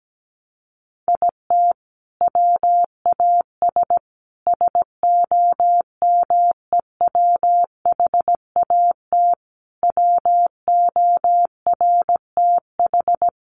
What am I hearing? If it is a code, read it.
Morse code, heard as a keyed tone, about 17 wpm, 700 Hz. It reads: IT WAS SOMEWHAT WORTH
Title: Sentences from Top 1300 Words 17wpm